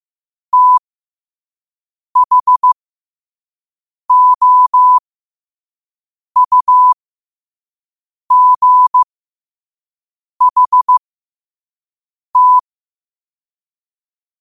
Morse code, heard as a keyed tone, about 15 wpm, spaced as for 5 wpm, 1000 Hz. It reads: THOUGHT
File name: qotdfiles / 5wpm